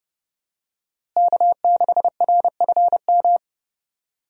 Morse code, heard as a keyed tone, about 30 wpm, 700 Hz. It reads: K6RFM
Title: USA Call Signs - 5 Characters - Repeated 2x - 30wpm